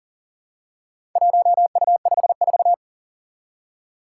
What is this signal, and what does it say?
Morse code, heard as a keyed tone, about 40 wpm, 700 Hz. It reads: 1U54
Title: Four Challenging Letter-Number Combinations 40wpm